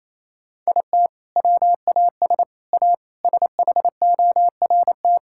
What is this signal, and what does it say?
Morse code, heard as a keyed tone, about 28 wpm, 700 Hz. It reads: IT WAS A SHORT
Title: Sentences from Top 1600 Words 28wpm